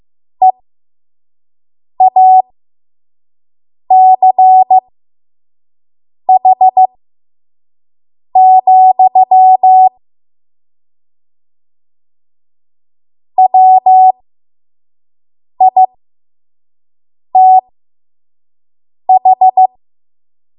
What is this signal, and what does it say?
Morse code, heard as a keyed tone, about 15 wpm, spaced as for 5 wpm, 750 Hz. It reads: EACH, WITH
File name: Morse / 05WPM